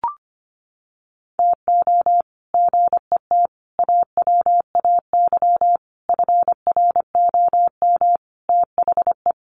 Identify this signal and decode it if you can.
Morse code, heard as a keyed tone, about 25 wpm, 700 Hz. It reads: TO GET AWAY FROM THE
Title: Sets of 5 Words Encore2 25wpm